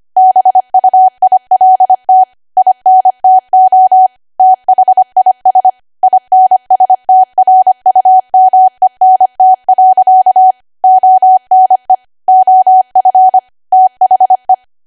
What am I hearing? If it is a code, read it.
Morse code, heard as a keyed tone, about 25 wpm, 750 Hz. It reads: BUILT INTO THIS INSTRUMENT. ONE OF THE